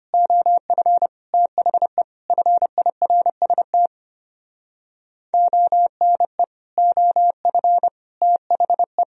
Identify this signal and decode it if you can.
Morse code, heard as a keyed tone, about 32 wpm, 700 Hz. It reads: OF THE FIRST ONE OF THE